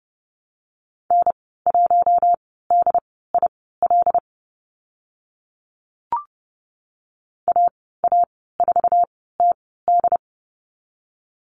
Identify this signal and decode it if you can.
Morse code, heard as a keyed tone, about 30 wpm, spaced as for 17 wpm, 700 Hz. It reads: N1DIL AA4TD
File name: Lesson 14g - Number 2 - Call Signs 30wpm